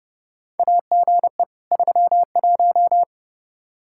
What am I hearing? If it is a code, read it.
Morse code, heard as a keyed tone, about 32 wpm, 700 Hz. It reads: AGE 31
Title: Practice QSO - Exchange 3 - More Info - 30wpm